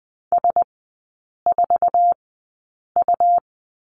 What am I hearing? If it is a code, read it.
Morse code, heard as a keyed tone, about 20 wpm, spaced as for 8 wpm, 690 Hz. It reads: S4U
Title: 133 Lesson 22e - Question Mark - Three Characters 20wpm